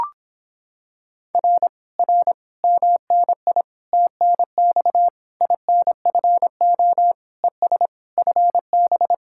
Morse code, heard as a keyed tone, about 26 wpm, 700 Hz. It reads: R R MNI TNX INFO ES FB